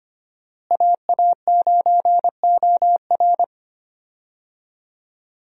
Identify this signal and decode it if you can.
Morse code, heard as a keyed tone, about 25 wpm, 700 Hz. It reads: AA9OR
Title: USA Call Signs - 5 Characters - Repeated 3x - 25wpm